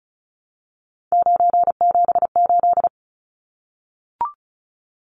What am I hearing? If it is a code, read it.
Morse code, heard as a keyed tone, about 34 wpm, 700 Hz. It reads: 978
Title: Numbers - 3-Digit 35wpm